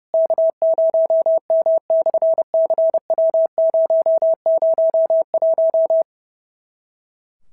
Morse code, heard as a keyed tone, about 30 wpm, 650 Hz. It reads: K0M/CW001
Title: SOTA QSO Practice 30wpm